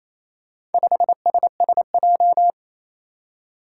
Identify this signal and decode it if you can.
Morse code, heard as a keyed tone, about 29 wpm, 700 Hz. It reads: 5SSJ
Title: Four Challenging Letter-Number Combinations 28wpm